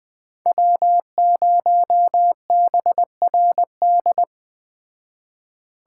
Morse code, heard as a keyed tone, about 20 wpm, 700 Hz. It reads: W0BRD
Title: Speed Racing - USA 1x3 Call Signs 17wpm